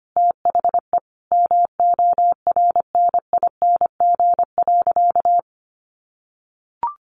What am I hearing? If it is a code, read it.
Morse code, heard as a keyed tone, about 25 wpm, 700 Hz. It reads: THE MORNING.
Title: Sentences from Top 1500 Words 25wpm